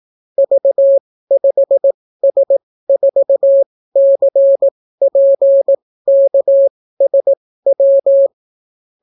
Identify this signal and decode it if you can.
Morse code, heard as a keyed tone, about 18 wpm, spaced as for 14 wpm, 550 Hz. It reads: V5S4CPKSW